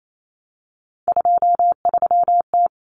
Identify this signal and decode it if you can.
Morse code, heard as a keyed tone, about 28 wpm, 700 Hz. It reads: 23T
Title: Common Cut Numbers - 3-Digit 28wpm